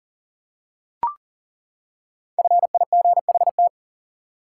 Morse code, heard as a keyed tone, about 40 wpm, 700 Hz. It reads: FIGHT
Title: Five Letter Words - 40wpm